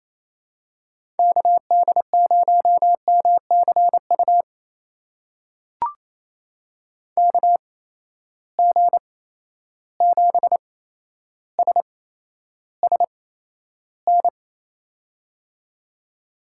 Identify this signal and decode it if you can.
Morse code, heard as a keyed tone, about 28 wpm, spaced as for 12 wpm, 700 Hz. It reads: KD0MCU KG7SSN
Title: USA Call Signs - 6 Characters - 8x Character Spacing - 28wpm